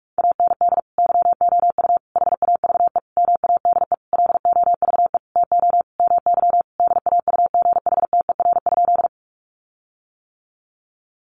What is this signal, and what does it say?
Morse code, heard as a keyed tone, about 44 wpm, 700 Hz. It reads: AND YOU HAVE MADE LOVE TO MY DAUGHTER?
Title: Sentences from Top 1500 Words 45wpm